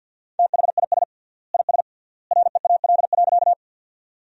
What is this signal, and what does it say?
Morse code, heard as a keyed tone, about 50 wpm, 700 Hz. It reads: THIS IS REAL.
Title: 3x Word-Spacing - Sentences from Top 300 Words - 50wpm